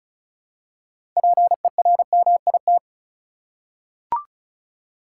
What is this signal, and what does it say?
Morse code, heard as a keyed tone, about 34 wpm, 700 Hz. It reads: PERMIT
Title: Six Letter Words - 35wpm